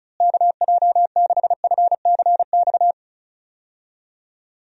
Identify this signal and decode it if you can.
Morse code, heard as a keyed tone, about 34 wpm, 700 Hz. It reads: KJ6FCX
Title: USA Call Signs - 6 Characters - Repeated 3x - 35wpm